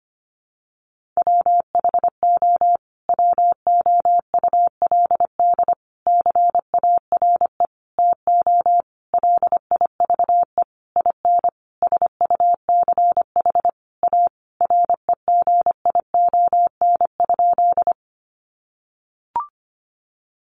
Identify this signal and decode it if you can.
Morse code, heard as a keyed tone, about 25 wpm, 700 Hz. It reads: WHO WOULD CARE TO LIVE IN SUCH A REGION?
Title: Sentences from Top 1500 Words 25wpm